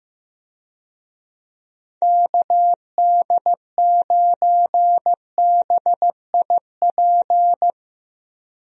Morse code, heard as a keyed tone, about 15 wpm, 700 Hz. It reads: KD9BIP
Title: USA Call Signs - 6 Characters - 15wpm